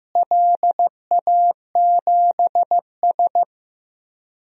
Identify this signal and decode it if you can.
Morse code, heard as a keyed tone, about 15 wpm, 700 Hz. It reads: LA7S
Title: DXCC-100 Call Signs - 4 Characters - Repeated 3x - 15wpm